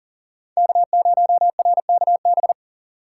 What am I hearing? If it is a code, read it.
Morse code, heard as a keyed tone, about 52 wpm, 700 Hz. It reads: K0RKB